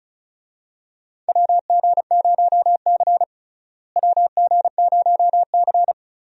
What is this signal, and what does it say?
Morse code, heard as a keyed tone, about 36 wpm, 700 Hz. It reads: WG0C WG0C
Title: USA Call Signs - 4 Characters - Repeated 2x - 35wpm